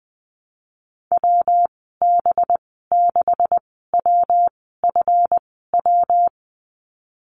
Morse code, heard as a keyed tone, about 20 wpm, spaced as for 15 wpm, 700 Hz. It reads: WB6WFW
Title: 193 Lesson 31g - Letter v - Call Signs 20wpm